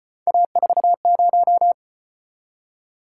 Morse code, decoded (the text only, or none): A40